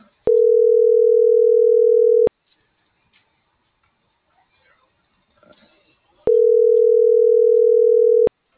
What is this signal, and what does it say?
Ringback tone (North America), 440 + 480 Hz.